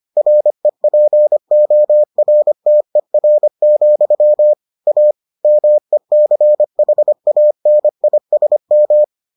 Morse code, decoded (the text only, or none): REPORTER, A MECHANISM